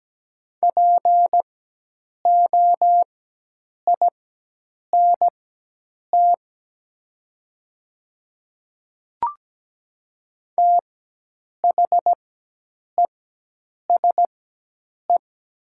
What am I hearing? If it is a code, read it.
Morse code, heard as a keyed tone, about 17 wpm, spaced as for 8 wpm, 700 Hz. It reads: POINT THESE
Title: 4x Character Spacing - Top 200 Words - 17wpm